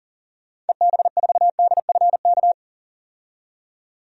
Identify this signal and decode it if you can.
Morse code, heard as a keyed tone, about 40 wpm, 700 Hz. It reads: EB4DFK
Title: DXCC-100 Call Signs - 6 Characters - 40wpm